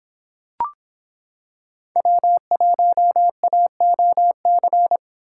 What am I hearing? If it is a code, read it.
Morse code, heard as a keyed tone, about 26 wpm, 700 Hz. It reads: W1AOC